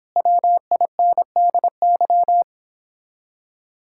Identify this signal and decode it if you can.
Morse code, heard as a keyed tone, about 29 wpm, 700 Hz. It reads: WINDY